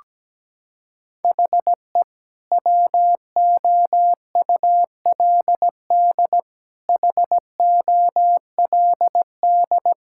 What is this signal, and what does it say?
Morse code, heard as a keyed tone, about 17 wpm, 700 Hz. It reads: HE WOULD HOLD
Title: Sentences from Top 1000 Words - Review Part 6 - 17wpm